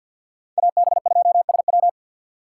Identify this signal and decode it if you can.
Morse code, heard as a keyed tone, about 51 wpm, 700 Hz. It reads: AB2SW
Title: USA Call Signs - 5 Characters - 2x Character Spacing - 50wpm